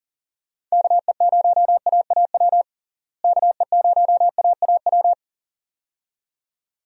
Morse code, heard as a keyed tone, about 40 wpm, 700 Hz. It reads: KE0AAW KE0AAW
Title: USA Call Signs - 6 Characters - Repeated 2x - 40wpm